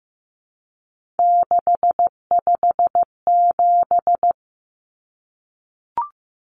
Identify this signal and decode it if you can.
Morse code, heard as a keyed tone, about 15 wpm, 700 Hz. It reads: 657
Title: Common Cut Numbers - 3-Digit 15wpm